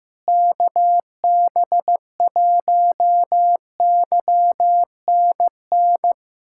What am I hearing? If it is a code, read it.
Morse code, heard as a keyed tone, about 15 wpm, 700 Hz. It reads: KB1YNN
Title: USA Callsign DE Callsign - 6 Characters - 15wpm